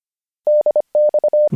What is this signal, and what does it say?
Morse code, heard as a keyed tone, about 25 wpm, 600 Hz. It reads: DX